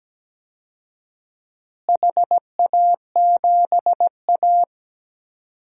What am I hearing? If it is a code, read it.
Morse code, heard as a keyed tone, about 17 wpm, 700 Hz. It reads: HA7A
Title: DXCC-100 Call Signs - 4 Characters - Repeated 3x - 17wpm